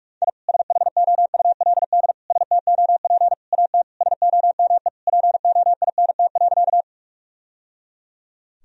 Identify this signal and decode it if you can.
Morse code, heard as a keyed tone, about 42 wpm, 700 Hz. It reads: I SHOULD STOP AT SOME POINT.
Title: Sentences from Top 1000 Words - Review Part 1 - 45wpm